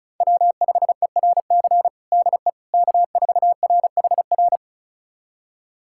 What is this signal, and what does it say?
Morse code, heard as a keyed tone, about 36 wpm, 700 Hz. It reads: W5ERC DE K4RHR
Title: USA Callsign DE Callsign 35wpm